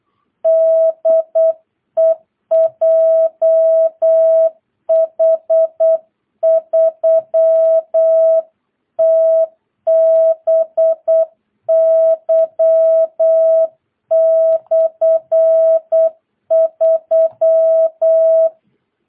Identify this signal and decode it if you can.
Morse code, heard as a keyed tone, about 8 wpm, 650 Hz. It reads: DE JH3TBY/3
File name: jh8cbh / 2022cw